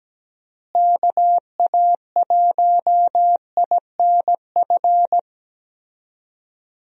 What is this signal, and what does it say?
Morse code, heard as a keyed tone, about 17 wpm, 700 Hz. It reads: KA1INF
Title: USA Call Signs - 6 Characters - Repeated 3x - 17wpm